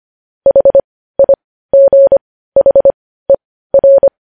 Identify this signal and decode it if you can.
Morse code, heard as a keyed tone, about 25 wpm, spaced as for 15 wpm, 550 Hz. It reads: HIGHER
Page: News Headlines in Morse Code at 15 WPM